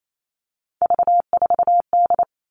Morse code, heard as a keyed tone, about 28 wpm, 700 Hz. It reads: V4D
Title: All Cut Numbers - 3-Digit 28wpm